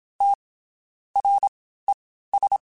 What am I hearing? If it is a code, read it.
Morse code, heard as a keyed tone, about 27 wpm, 800 Hz. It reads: T R E S
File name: Basic Letter Groups By7Groups / LFCYQ / LFCYQ@25